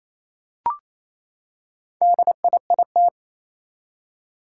Morse code, heard as a keyed tone, about 27 wpm, 700 Hz. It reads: DIIT
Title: Four Challenging Letter-Number Combinations 28wpm